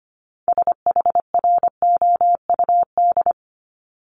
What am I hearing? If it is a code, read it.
Morse code, heard as a keyed tone, about 25 wpm, 700 Hz. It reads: SHROUD
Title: Multi-letter Phonemes Repeated 3x 25wpm